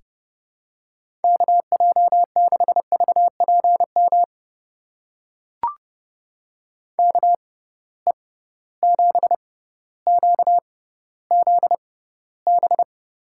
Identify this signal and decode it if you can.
Morse code, heard as a keyed tone, about 30 wpm, spaced as for 15 wpm, 700 Hz. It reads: KJ6VPM KE7QZB
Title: USA Call Signs - 6 Characters - 6x Character Spacing - 30wpm